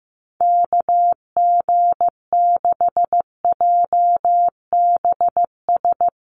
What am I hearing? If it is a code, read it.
Morse code, heard as a keyed tone, about 15 wpm, 700 Hz. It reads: KG6JBS
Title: USA Call Signs Repeated 3x 15wpm